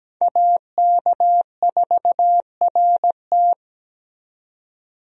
Morse code, heard as a keyed tone, about 17 wpm, 700 Hz. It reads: AK4RT
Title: USA Call Signs - 5 Characters - 17wpm